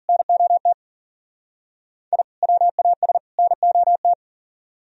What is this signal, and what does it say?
Morse code, heard as a keyed tone, about 50 wpm, 700 Hz. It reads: NOT I WAS NOT
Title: Speed-Racing - Sets of 3 Words 40wpm